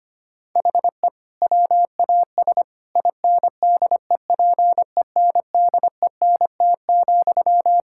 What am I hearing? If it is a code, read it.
Morse code, heard as a keyed tone, about 25 wpm, 700 Hz. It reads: HE WAS INDEPENDENT,
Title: Sentences from Top 1000 Words - Review Part 6 - 25wpm